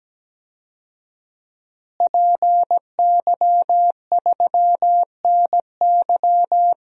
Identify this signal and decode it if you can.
Morse code, heard as a keyed tone, about 17 wpm, 700 Hz. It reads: PY3NY
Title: DXCC-100 Call Signs - 5 Characters - Repeated 2x - 17wpm